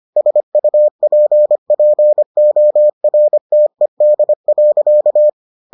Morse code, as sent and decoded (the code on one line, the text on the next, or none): ... ..- .--. .--. --- .-. - . -.. .-.-.-
SUPPORTED.